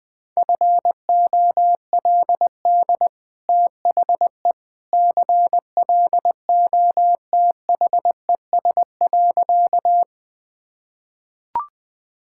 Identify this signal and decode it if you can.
Morse code, heard as a keyed tone, about 20 wpm, 700 Hz. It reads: FOLD THE CLOTHES.
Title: Easy Sentences with Hints - 20wpm